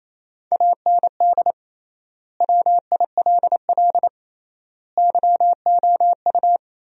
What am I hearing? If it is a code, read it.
Morse code, heard as a keyed tone, about 28 wpm, 700 Hz. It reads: AND WILL YOU
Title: 3x Word-Spacing - Sentences from Top 400 Words - 28wpm